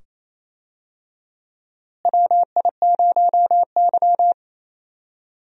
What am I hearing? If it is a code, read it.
Morse code, heard as a keyed tone, about 28 wpm, 700 Hz. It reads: WI0Y
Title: USA Call Signs - 4 Characters - Repeated 2x - 28wpm